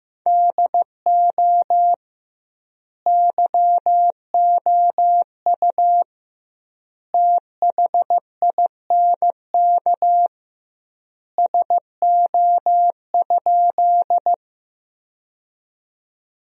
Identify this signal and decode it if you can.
Morse code, heard as a keyed tone, about 15 wpm, 700 Hz. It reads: DO YOU THINK SO?
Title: 2x Word-Spacing - Sentences from Top 200 Words - 15wpm